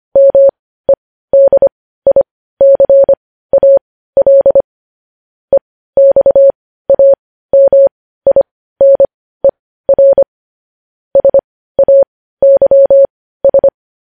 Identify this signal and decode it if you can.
Morse code, heard as a keyed tone, about 25 wpm, spaced as for 15 wpm, 550 Hz. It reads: MEDICAL EXAMINER SAYS
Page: news-headlines-in-morse-code-at-15-wpm